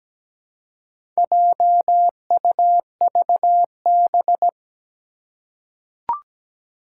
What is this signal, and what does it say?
Morse code, heard as a keyed tone, about 17 wpm, 700 Hz. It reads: JUVB
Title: Four Challenging Letter-Number Combinations 17wpm